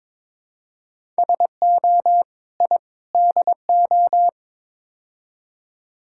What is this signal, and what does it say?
Morse code, heard as a keyed tone, about 22 wpm, 700 Hz. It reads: SO I DO
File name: practice / 22wpm / Build A Phrase - 5-Word Phrases - 22wpm